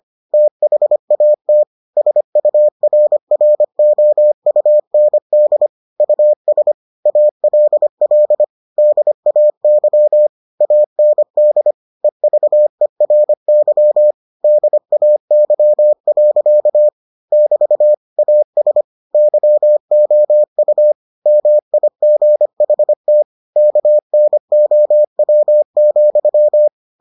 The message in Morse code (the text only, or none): THAT SURROUND US ALL DAY AND EVERY DAY. <BT> AS YOU MIGHT KNOW,